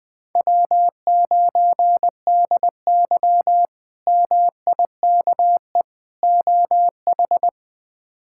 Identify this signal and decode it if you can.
Morse code, heard as a keyed tone, about 20 wpm, 700 Hz. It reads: W9DY MIKE OH
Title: K1USN SST QSO Practice - 20wpm